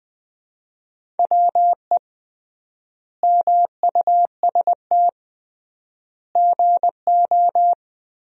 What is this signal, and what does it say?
Morse code, heard as a keyed tone, about 20 wpm, 700 Hz. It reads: WE MUST GO
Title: 3x Word-Spacing - Sentences from Top 300 Words - 20wpm